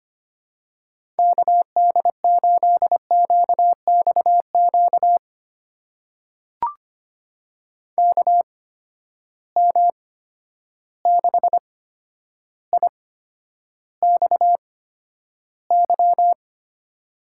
Morse code, heard as a keyed tone, about 25 wpm, spaced as for 11 wpm, 700 Hz. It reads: KD8QXQ KM6IXY
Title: USA Call Signs - 6 Characters - 8x Character Spacing - 25wpm